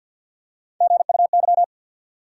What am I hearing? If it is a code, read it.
Morse code, heard as a keyed tone, about 52 wpm, 700 Hz. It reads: GUY
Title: Three Letter Words - 50wpm